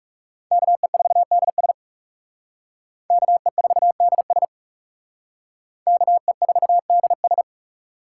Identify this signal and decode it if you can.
Morse code, heard as a keyed tone, about 34 wpm, 700 Hz. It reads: KE4DS KE4DS KE4DS
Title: Speed Racing - USA 2x2 Call Signs 30wpm